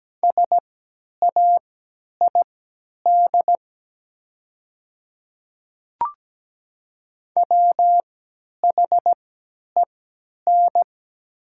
Morse code, heard as a keyed tone, about 17 wpm, spaced as for 10 wpm, 700 Hz. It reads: SAID WHEN
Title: 3x Character Spacing - Top 100 Words - 17wpm